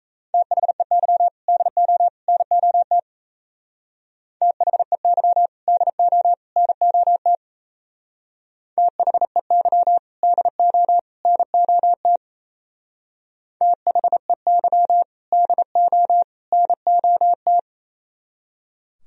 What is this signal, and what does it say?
Morse code, heard as a keyed tone, about 34 wpm, 700 Hz. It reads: THEY DO NOT THEY DO NOT THEY DO NOT THEY DO NOT
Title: Speed-Racing - Sets of 3 Words 28wpm